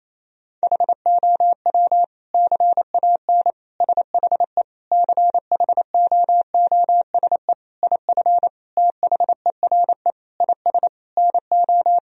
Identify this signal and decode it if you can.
Morse code, heard as a keyed tone, about 28 wpm, 700 Hz. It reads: HOW CAN SHE CHOOSE IF THERE IS NO
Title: Sentences from Top 1000 Words - Review Part 4 - 28wpm